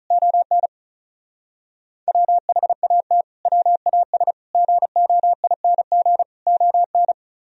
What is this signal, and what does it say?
Morse code, heard as a keyed tone, about 32 wpm, 700 Hz. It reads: ON WHAT WAS GOING ON
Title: Speed-Racing - Sets of 4 Words 35wpm